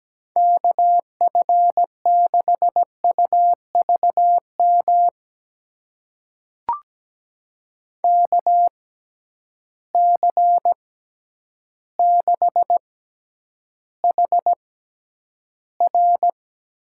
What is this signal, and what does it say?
Morse code, heard as a keyed tone, about 17 wpm, spaced as for 9 wpm, 700 Hz. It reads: KF6UVM KC6HR
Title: USA Call Signs - 6 Characters - 6x Character Spacing - 17wpm